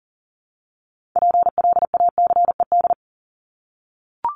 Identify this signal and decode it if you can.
Morse code, heard as a keyed tone, about 40 wpm, 700 Hz. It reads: PLACED
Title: ED Suffix - Top 100 Words with Base Words 40wpm